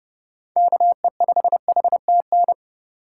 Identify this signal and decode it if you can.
Morse code, heard as a keyed tone, about 32 wpm, 700 Hz. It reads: KE5HTN
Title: USA Call Signs - 6 Characters - 3x Character Spacing - 30wpm